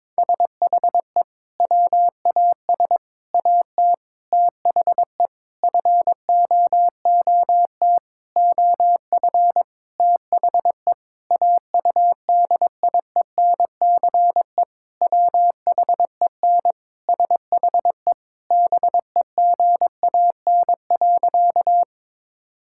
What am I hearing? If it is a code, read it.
Morse code, heard as a keyed tone, about 22 wpm, 700 Hz. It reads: SHE WAS AT THE FOOT OF THE AUDIENCE WHEN SHE BEGAN.
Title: Sentences from Top 1000 Words - Review Part 5 - 22wpm